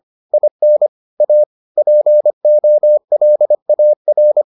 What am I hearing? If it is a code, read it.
Morse code, heard as a keyed tone, about 25 wpm, 600 Hz. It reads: IN A POLAR